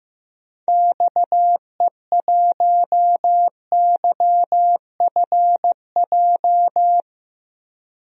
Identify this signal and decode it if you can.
Morse code, heard as a keyed tone, about 15 wpm, 700 Hz. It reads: XE1YFJ
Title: DXCC-100 Call Signs - 6 Characters - 15wpm